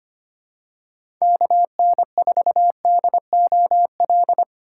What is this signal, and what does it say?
Morse code, heard as a keyed tone, about 25 wpm, 700 Hz. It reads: KN4DOL
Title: USA Call Signs - 4x Character Spacing - 25wpm